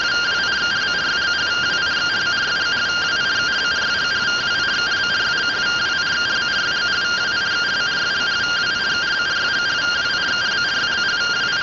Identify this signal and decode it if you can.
Two-tone teleprinter signal, about 170 Hz shift.